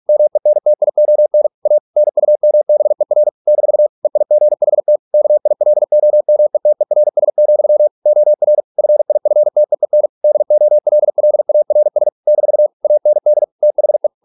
Morse code, heard as a keyed tone, about 46 wpm, 600 Hz. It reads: MENTION A NUMBER - EIGHT KILOMETERS, OR FIFTEEN DOLLARS - AND THE